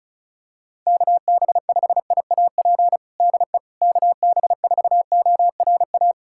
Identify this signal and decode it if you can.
Morse code, heard as a keyed tone, about 36 wpm, 700 Hz. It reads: KB5IAP DE KB4ORA
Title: USA Callsign DE Callsign - 6 Characters - 35wpm